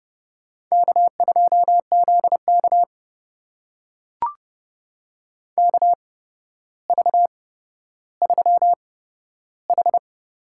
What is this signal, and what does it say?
Morse code, heard as a keyed tone, about 30 wpm, spaced as for 13 wpm, 700 Hz. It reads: K2ZK KV3H
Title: USA Call Signs - 4 Characters - 8x Character Spacing - 30wpm